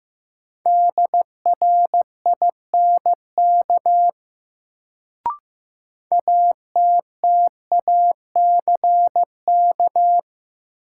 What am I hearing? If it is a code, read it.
Morse code, heard as a keyed tone, about 15 wpm, 700 Hz. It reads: DRINK ATTACK
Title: Sentences from Top 1500 Words - New Word Drill - 15wpm